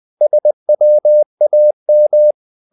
Morse code, heard as a keyed tone, about 20 wpm, 600 Hz. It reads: SWAM
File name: the old man and the sea part 2 20wpm